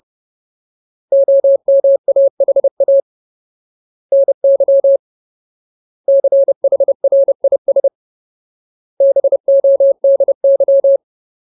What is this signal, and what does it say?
Morse code, heard as a keyed tone, about 30 wpm, 550 Hz. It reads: OMAHA NY CHRIS BODY
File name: Ultimate Word List / 30WPM Ultimate